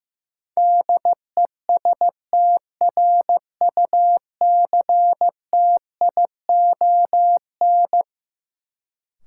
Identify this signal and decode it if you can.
Morse code, heard as a keyed tone, about 15 wpm, 700 Hz. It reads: DESTRUCTION
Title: Sets of 5 Words Encore - New Word Drill - 15wpm